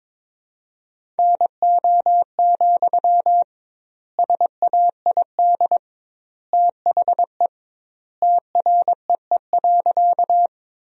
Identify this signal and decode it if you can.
Morse code, heard as a keyed tone, about 22 wpm, 700 Hz. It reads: NO, SAID THE TREE.
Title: 2x Word-Spacing - Sentences from Top 300 Words - 22wpm